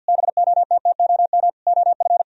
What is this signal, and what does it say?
Morse code, heard as a keyed tone, about 60 wpm, 700 Hz. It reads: BOTTOM OF
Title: Speed-Racing - Sets of 4 Words 50wpm